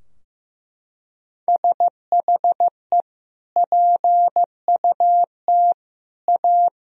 Morse code, heard as a keyed tone, about 15 wpm, 700 Hz. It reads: SHE PUT A
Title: Sentences from Top 1000 Words - Review Part 2 - 15wpm